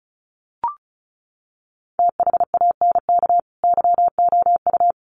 Morse code, heard as a keyed tone, about 34 wpm, 700 Hz. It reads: THANK YOU
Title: Sets of 2 Words 35wpm